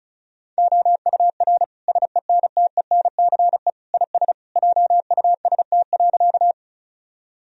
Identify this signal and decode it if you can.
Morse code, heard as a keyed tone, about 34 wpm, 700 Hz. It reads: OUR SENTENCE IS JUST.
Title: Sentences from Top 1000 Words - Review Part 1 - 35wpm